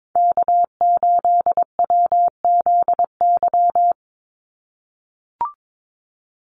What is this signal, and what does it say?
Morse code, heard as a keyed tone, about 22 wpm, 700 Hz. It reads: K8WZY
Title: USA Call Signs 22wpm